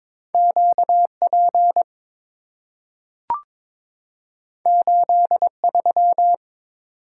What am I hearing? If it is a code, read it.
Morse code, heard as a keyed tone, about 22 wpm, 700 Hz. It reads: QP 83
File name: practice / 22wpm / Two Letter-Number Combinations - 22wpm